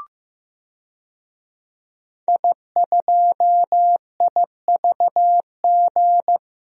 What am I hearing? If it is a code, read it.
Morse code, heard as a keyed tone, about 15 wpm, 700 Hz. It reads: I2IVG
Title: DXCC-100 Call Signs - 5 Characters - 15wpm